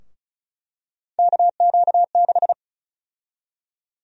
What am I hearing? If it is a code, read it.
Morse code, heard as a keyed tone, about 36 wpm, 700 Hz. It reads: KQ6
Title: USA Call Sign Prefixes - 35wpm